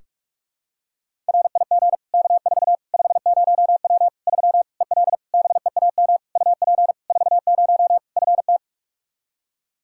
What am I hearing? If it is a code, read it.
Morse code, heard as a keyed tone, about 46 wpm, 700 Hz. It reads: RIG K4 50W 3 EL BEAM UP 40 FT
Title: Practice QSO - Exchange 2 - Info - 45wpm